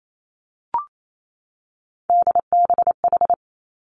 Morse code, heard as a keyed tone, about 28 wpm, 700 Hz. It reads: DBH
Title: Three Challenging Letter-Number Combinations 28wpm